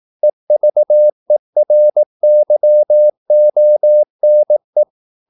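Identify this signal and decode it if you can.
Morse code, heard as a keyed tone, about 18 wpm, 600 Hz. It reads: EVERYONE
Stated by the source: Morse practice at 18 wpm, English text, from CW med Gnister.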